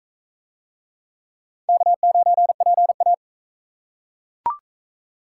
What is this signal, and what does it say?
Morse code, heard as a keyed tone, about 37 wpm, 700 Hz. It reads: K9PA